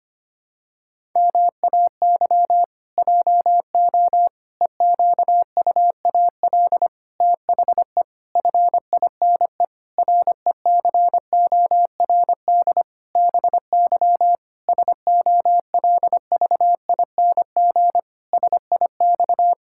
Morse code, heard as a keyed tone, about 25 wpm, 700 Hz. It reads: MAY JO EQUAL THE FINE RECORD BY SOLVING SIX
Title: Pangram Challenge v2 - 25wpm